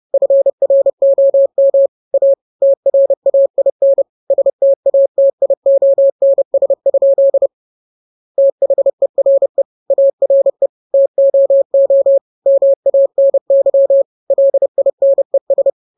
Morse code, heard as a keyed tone, about 30 wpm, 550 Hz. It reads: FROM A TRAIN STATIONS? THERE ARE TOO MANY LINES
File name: jokes / joke 20 30 0000